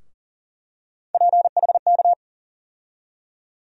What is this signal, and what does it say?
Morse code, heard as a keyed tone, about 38 wpm, 700 Hz. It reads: PHK
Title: USA Call Sign Postfixes - 3 Letters 40wpm